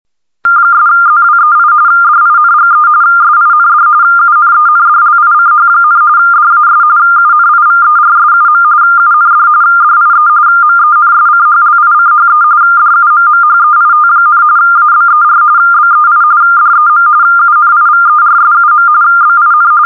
RTTY, about 45.45 baud. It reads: 'ZWEI DINGE SIND UNENDLICH, DAS UNIVERSUM UND DIE MENSCHLICHE DUMMHEIT, ABER BEI DEM UNIVERSUM BIN